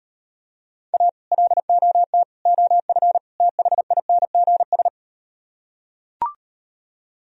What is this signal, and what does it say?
Morse code, heard as a keyed tone, about 38 wpm, 700 Hz. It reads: A LOT OF THINGS